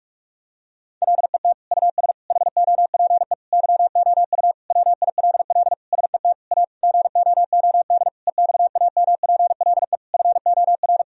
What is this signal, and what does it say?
Morse code, heard as a keyed tone, about 43 wpm, 700 Hz. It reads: LET US HOPE YOU WILL SET A GOOD EXAMPLE FOR
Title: Sentences from Top 1000 Words - Review Part 1 - 45wpm